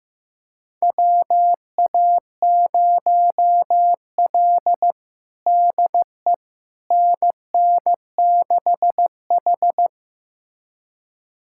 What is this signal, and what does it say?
Morse code, heard as a keyed tone, about 15 wpm, 700 Hz. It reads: WA0L DE NN6H
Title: USA Callsign DE Callsign - 4 Characters - 15wpm